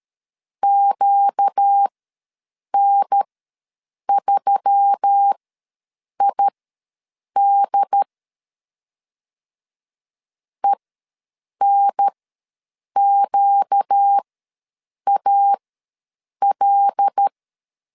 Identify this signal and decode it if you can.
Morse code, heard as a keyed tone, about 13 wpm, spaced as for 7 wpm, 800 Hz. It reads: QN3ID ENQAL
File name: LEZIONI / 5 LBVC / 5 LB ws 12wpm 3034